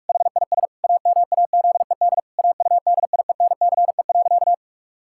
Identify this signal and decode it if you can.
Morse code, heard as a keyed tone, about 41 wpm, 700 Hz. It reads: HIS AMAZED AUDIENCE.
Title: Pangram Challenge v2 - 45wpm